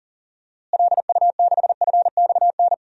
Morse code, heard as a keyed tone, about 40 wpm, 700 Hz. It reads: LU6FXN
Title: DXCC-100 Call Signs - 6 Characters - Repeated 2x - 40wpm